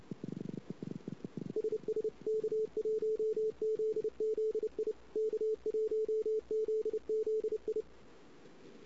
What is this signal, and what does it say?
Morse code, heard as a keyed tone, about 30 wpm, 440 Hz. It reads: SS K1ZZI K1ZZI
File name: CW / 2008-ss-cw-k5tr-op-n5rz -16:00